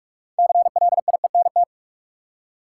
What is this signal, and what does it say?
Morse code, heard as a keyed tone, about 46 wpm, 700 Hz. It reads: CLIENT
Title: Six Letter Words - 45wpm